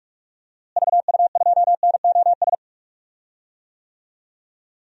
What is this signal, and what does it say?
Morse code, heard as a keyed tone, about 43 wpm, 700 Hz. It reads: VU2NOS
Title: DXCC-100 Call Signs - 6 Characters - Repeated 2x - 45wpm